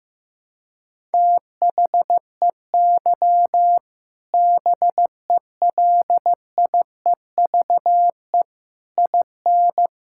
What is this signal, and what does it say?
Morse code, heard as a keyed tone, about 15 wpm, 700 Hz. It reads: THEY BELIEVE IN
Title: Sentences from Top 1000 Words - Review Part 5 - 15wpm